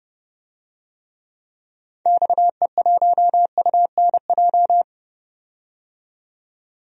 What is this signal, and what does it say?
Morse code, heard as a keyed tone, about 31 wpm, 700 Hz. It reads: XE1UNJ
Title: DXCC-100 Call Signs - 6 Characters - 30wpm